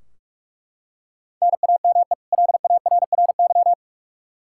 Morse code, heard as a keyed tone, about 43 wpm, 700 Hz. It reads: NAME LARRY
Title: Practice QSO - Exchange 1 - Beginning - 45wpm